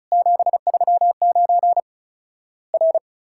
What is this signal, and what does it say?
Morse code, heard as a keyed tone, about 36 wpm, 700 Hz. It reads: 739 R
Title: Worked All Europe DX Contest - QTC Exchange - 35wpm